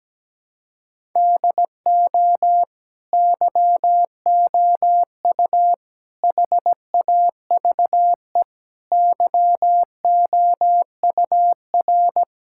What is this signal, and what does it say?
Morse code, heard as a keyed tone, about 17 wpm, 700 Hz. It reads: DO YOU HAVE YOUR
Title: Sentences from Top 1000 Words - Review Part 5 - 17wpm